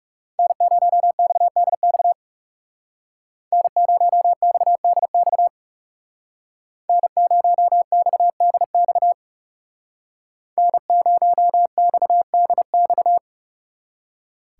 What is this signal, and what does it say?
Morse code, heard as a keyed tone, about 38 wpm, 700 Hz. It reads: N0XDX N0XDX N0XDX N0XDX